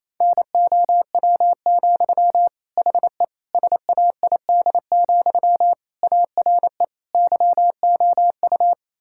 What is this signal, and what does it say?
Morse code, heard as a keyed tone, about 27 wpm, 700 Hz. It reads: NOW, HE SAID, ARE YOU
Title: Sentences from Top 1600 Words 28wpm